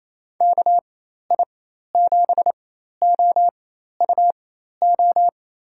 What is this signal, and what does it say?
Morse code, heard as a keyed tone, about 28 wpm, spaced as for 13 wpm, 700 Hz. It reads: KI7OUO
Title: USA Call Signs - 4x Character Spacing - 28wpm